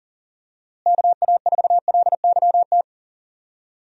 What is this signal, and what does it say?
Morse code, heard as a keyed tone, about 38 wpm, 700 Hz. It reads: KA4LYT
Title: USA Call Signs - 6 Characters - Repeated 3x - 40wpm